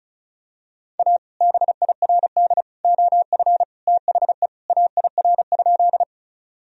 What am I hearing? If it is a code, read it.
Morse code, heard as a keyed tone, about 36 wpm, 700 Hz. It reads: A BIRD OF THE AIR?
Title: Sentences from Top 1000 Words - Review Part 2 - 35wpm